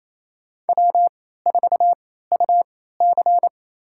Morse code, heard as a keyed tone, about 28 wpm, spaced as for 16 wpm, 700 Hz. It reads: W4UC
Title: USA Call Signs - 4 Characters - 3x Character Spacing - 28wpm